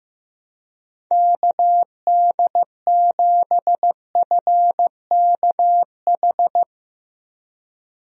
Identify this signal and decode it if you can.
Morse code, heard as a keyed tone, about 15 wpm, 700 Hz. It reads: KD7FKH
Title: USA Call Signs - 6 Characters - Repeated 2x - 15wpm